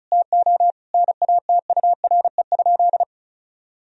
Morse code, heard as a keyed tone, about 34 wpm, 700 Hz. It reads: TO NATURE?
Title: Sentences from Top 1000 Words - Review Part 4 - 35wpm